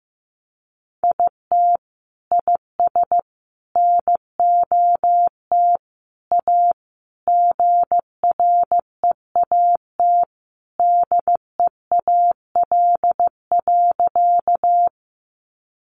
Morse code, heard as a keyed tone, about 15 wpm, 700 Hz. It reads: IT IS NOT A GREAT DEAL.
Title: Sentences from Top 800 Words 15wpm